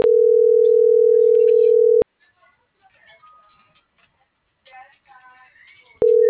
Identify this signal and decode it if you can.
Ringback tone (North America), 440 + 480 Hz.
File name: CENTRAL FRALSE / out aidee.cruz